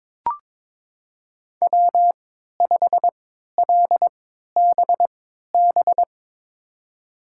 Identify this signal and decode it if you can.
Morse code, heard as a keyed tone, about 21 wpm, spaced as for 12 wpm, 700 Hz. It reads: W5LBB
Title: USA Call Signs - 3x Character Spacing - 22wpm